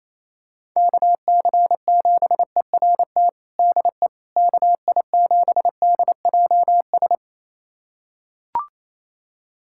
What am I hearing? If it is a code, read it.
Morse code, heard as a keyed tone, about 28 wpm, 700 Hz. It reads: KC7ERT DE KI7DJS
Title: USA Callsign DE Callsign - 6 Characters - 28wpm